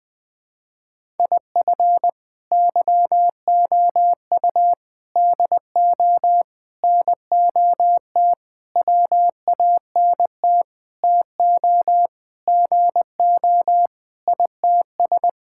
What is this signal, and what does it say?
Morse code, heard as a keyed tone, about 20 wpm, 700 Hz. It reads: IF YOU DO NOT WANT TO GO ITS
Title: Sentences with 5-Word Phrases - 20wpm